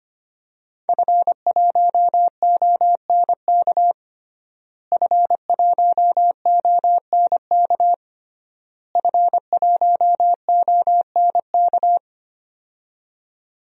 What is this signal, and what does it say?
Morse code, heard as a keyed tone, about 25 wpm, 700 Hz. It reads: F1ONK F1ONK F1ONK
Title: DXCC-100 Call Signs - 5 Characters - Repeated 3x - 25wpm